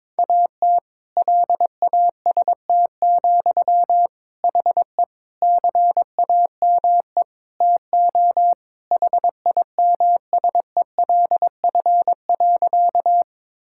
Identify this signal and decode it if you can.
Morse code, heard as a keyed tone, about 21 wpm, 700 Hz. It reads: AT LAST, HE CAME TO HIMSELF.
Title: Sentences from Top 1600 Words 22wpm